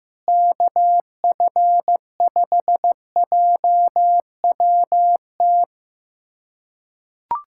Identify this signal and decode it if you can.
Morse code, heard as a keyed tone, about 15 wpm, 700 Hz. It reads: KF5JWT
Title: USA Callsign DE Callsign - 6 Characters - 15wpm